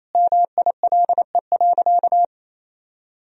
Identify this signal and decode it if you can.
Morse code, heard as a keyed tone, about 28 wpm, 700 Hz. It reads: MILE.
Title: Sentences from Top 1000 Words - Review Part 4 - 28wpm